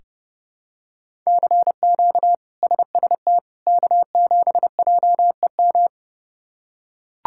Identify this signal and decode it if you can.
Morse code, heard as a keyed tone, about 30 wpm, 700 Hz. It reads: CQ SST K7JEM
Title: K1USN SST QSO Practice - 30wpm